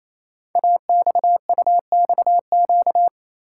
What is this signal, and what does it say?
Morse code, heard as a keyed tone, about 27 wpm, 700 Hz. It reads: AXUXQ
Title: Code Groups - 5 Letters - 28wpm